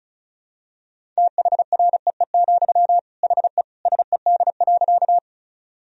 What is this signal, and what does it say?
Morse code, heard as a keyed tone, about 34 wpm, 700 Hz. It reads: THREE, HE SED.
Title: Sentences from Top 200 Words and Abbreviations 35wpm